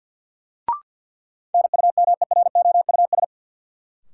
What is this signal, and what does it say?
Morse code, heard as a keyed tone, about 51 wpm, 700 Hz. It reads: NUMEROUS
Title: Sentences from Top 1400 Words - New Word Drill - 50wpm